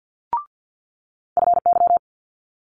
Morse code, heard as a keyed tone, about 50 wpm, 700 Hz. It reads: FY
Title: Multi-letter Phonemes Quick Fire 50wpm